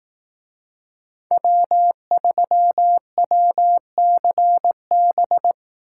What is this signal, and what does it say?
Morse code, heard as a keyed tone, about 18 wpm, 700 Hz. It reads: W3WCB
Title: Speed Racing - USA 1x3 Call Signs 15wpm